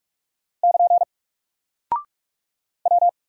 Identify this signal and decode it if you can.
Morse code, heard as a keyed tone, about 44 wpm, 700 Hz. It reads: <KN> W
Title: Single Character (Letter:Number:Punctuation:Prosign) 45wpm